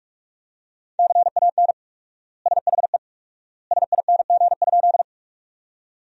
Morse code, heard as a keyed tone, about 41 wpm, 700 Hz. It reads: CAN SHE SING?
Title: 4x Word-Spacing - Sentences from Top 400 Words - 45wpm